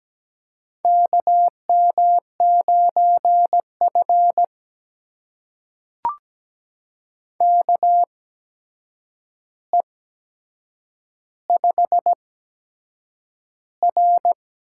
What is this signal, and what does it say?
Morse code, heard as a keyed tone, about 17 wpm, spaced as for 7 wpm, 700 Hz. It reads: KM9F KE5R
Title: USA Call Signs - 8x Character Spacing - 17wpm